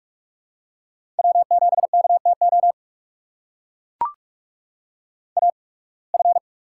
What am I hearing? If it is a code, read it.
Morse code, heard as a keyed tone, about 44 wpm, spaced as for 29 wpm, 700 Hz. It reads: W7KTO AF
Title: USA Call Signs - 8x Character Spacing - 45wpm